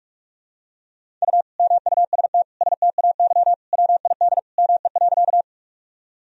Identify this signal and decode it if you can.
Morse code, heard as a keyed tone, about 42 wpm, 700 Hz. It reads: U MUST STAY WID ME.
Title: Sentences from Top 500 Words and Abbreviations 45wpm